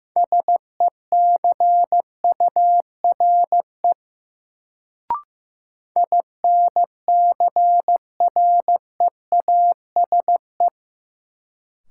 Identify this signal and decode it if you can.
Morse code, heard as a keyed tone, about 15 wpm, 700 Hz. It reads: SECURE INCREASE
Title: Sentences from Top 1200 Words - New Word Drill Alt - 15wpm